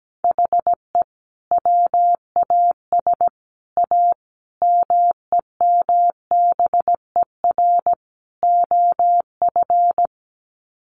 Morse code, heard as a keyed tone, about 17 wpm, 700 Hz. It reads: HE WAS A MEMBER OF
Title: Sets of 5 Words Encore2 17wpm